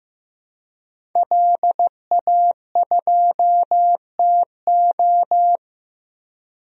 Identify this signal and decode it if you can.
Morse code, heard as a keyed tone, about 15 wpm, 700 Hz. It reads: LA2TO
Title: DXCC-100 Call Signs - 5 Characters - Repeated 2x - 15wpm